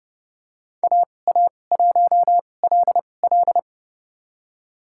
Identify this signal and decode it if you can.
Morse code, heard as a keyed tone, about 29 wpm, spaced as for 21 wpm, 700 Hz. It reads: AA1LL
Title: USA Call Signs - 5 Characters - 2x Character Spacing - 30wpm